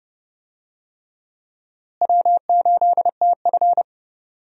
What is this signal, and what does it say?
Morse code, heard as a keyed tone, about 30 wpm, 700 Hz. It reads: W8TF
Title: USA Call Signs - 4 Characters - Repeated 3x - 30wpm